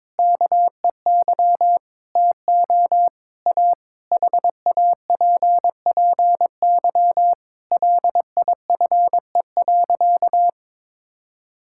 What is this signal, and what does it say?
Morse code, heard as a keyed tone, about 22 wpm, 700 Hz. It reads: KEY TO A HAPPY LIFE.
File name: Difficult Sentences with Hints - 22wpm